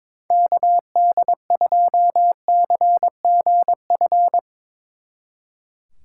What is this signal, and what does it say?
Morse code, heard as a keyed tone, about 23 wpm, 700 Hz. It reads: KD2CGF
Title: Speed Racing - USA 2x3 Call Signs 22wpm